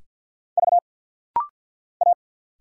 Morse code, heard as a keyed tone, about 50 wpm, 700 Hz. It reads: V A
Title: Single Letters - Rapid-Fire 50wpm